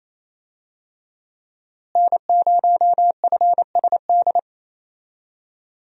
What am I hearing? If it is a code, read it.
Morse code, heard as a keyed tone, about 28 wpm, 700 Hz. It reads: N0FSD
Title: USA Call Signs - 5 Characters - Repeated 3x - 28wpm